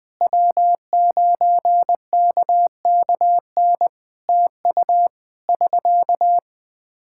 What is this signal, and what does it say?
Morse code, heard as a keyed tone, about 20 wpm, 700 Hz. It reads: W9KKN TU <SK>
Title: Practice QSO - Exchange 4 - Ending - 20wpm